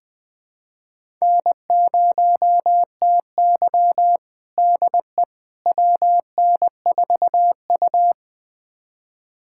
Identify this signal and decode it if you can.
Morse code, heard as a keyed tone, about 20 wpm, 700 Hz. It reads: N0TY DE WN4U
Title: USA Callsign DE Callsign - 4 Characters - 20wpm